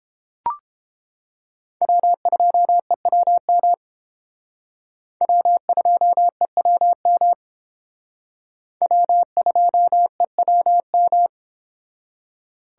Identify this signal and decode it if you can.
Morse code, heard as a keyed tone, about 29 wpm, 700 Hz. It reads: W2EWM W2EWM W2EWM